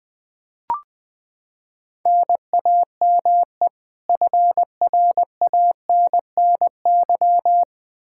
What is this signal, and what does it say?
Morse code, heard as a keyed tone, about 20 wpm, 700 Hz. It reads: NAME FRANNY
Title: Practice QSO - Exchange 1 - Beginning - 20wpm